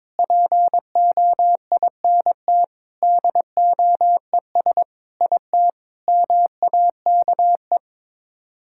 Morse code, heard as a keyed tone, about 22 wpm, 700 Hz. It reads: POINT DOES IT MAKE
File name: practice / 22wpm / Sets of 8 Words - 22wpm